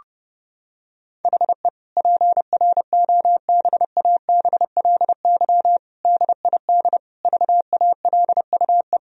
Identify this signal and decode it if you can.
Morse code, heard as a keyed tone, about 32 wpm, 700 Hz. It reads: HE PROBABLY DID VALUE
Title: Sentences from Top 1000 Words - Review Part 4 - 30wpm